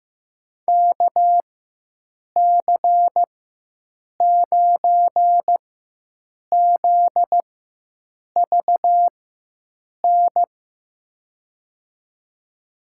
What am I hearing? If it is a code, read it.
Morse code, heard as a keyed tone, about 15 wpm, spaced as for 7 wpm, 700 Hz. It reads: KC9ZVN
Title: USA Call Signs - 6 Characters - 4x Character Spacing - 15wpm